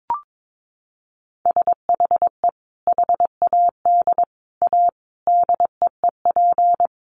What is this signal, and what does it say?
Morse code, heard as a keyed tone, about 22 wpm, 700 Hz. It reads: SHE HAD A DEEP
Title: Sentences from Top 1300 Words 22wpm